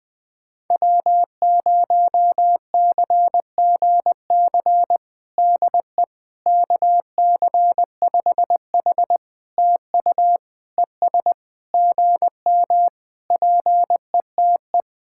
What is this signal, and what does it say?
Morse code, heard as a keyed tone, about 20 wpm, 700 Hz. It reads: W0CGC DE KC5H TU ES GM PETE
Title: POTA QSO Practice - 20wpm